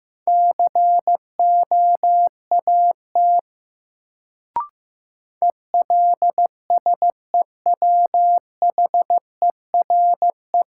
Morse code, heard as a keyed tone, about 15 wpm, 700 Hz. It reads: COAT ELSEWHERE
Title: Sentences from Top 1600 Words - New Word Drill - 15wpm